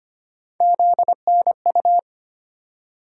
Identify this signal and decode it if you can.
Morse code, heard as a keyed tone, about 25 wpm, 700 Hz. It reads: ZNU
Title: USA Call Sign Postfixes - 3 Letters 25wpm